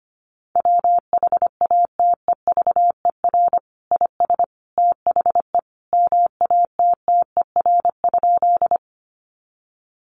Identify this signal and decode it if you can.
Morse code, heard as a keyed tone, about 25 wpm, 700 Hz. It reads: WHATEVER IS THE MATTER?
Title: Sentences from Top 1100 Words 25wpm